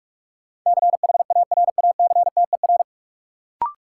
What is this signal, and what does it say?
Morse code, heard as a keyed tone, about 46 wpm, 700 Hz. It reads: CHARACTER
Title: Sentences from Top 1200 Words - New Word Drill Alt - 45wpm